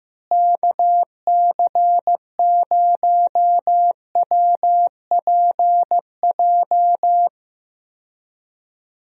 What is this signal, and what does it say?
Morse code, heard as a keyed tone, about 15 wpm, 700 Hz. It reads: KC0WPJ
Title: USA Call Signs - 6 Characters - Repeated 2x - 15wpm